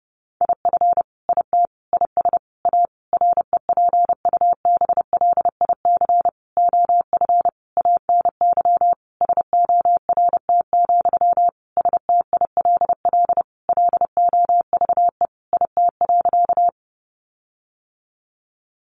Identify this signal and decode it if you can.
Morse code, heard as a keyed tone, about 28 wpm, 700 Hz. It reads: IF IT IS A REPUBLIC OF ANY SORT, STILL LOVE IT.
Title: Sentences from Top 1500 Words 30wpm